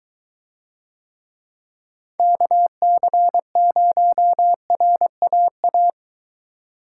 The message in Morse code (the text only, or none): KC0RAA